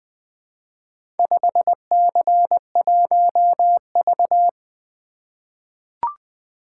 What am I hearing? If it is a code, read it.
Morse code, heard as a keyed tone, about 20 wpm, 700 Hz. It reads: 5C1V
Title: Four Letter-Number Combinations - 20wpm